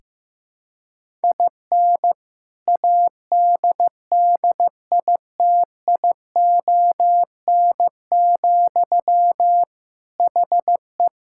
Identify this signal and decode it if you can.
Morse code, heard as a keyed tone, about 16 wpm, 700 Hz. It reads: IN ADDITION, HE
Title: Sentences from Top 1000 Words - Review Part 6 - 15wpm